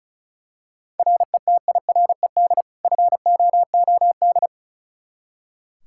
Morse code, heard as a keyed tone, about 36 wpm, 700 Hz. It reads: RETIRED FOOD
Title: Practice QSO - Exchange 3 - More Info - 35wpm